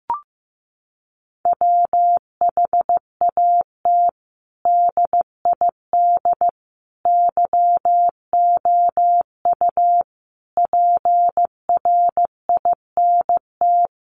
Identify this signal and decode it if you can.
Morse code, heard as a keyed tone, about 15 wpm, 700 Hz. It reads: WHAT DID YOU PRINT
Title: Sentences from Top 1400 Words 15wpm